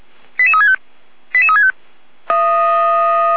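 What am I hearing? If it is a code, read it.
Five-tone selective call (ZVEI), heard as quick runs of five tones, tones of about 70 ms. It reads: 88365 88365
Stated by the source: ZVEI five-tone selective call received by SDR, on the server of Feuerwehr Orlamünde (fire brigade).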